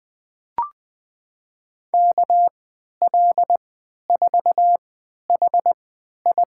Morse code, heard as a keyed tone, about 20 wpm, spaced as for 11 wpm, 700 Hz. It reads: KL4HI
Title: USA Call Signs - 5 Characters - 3x Character Spacing - 20wpm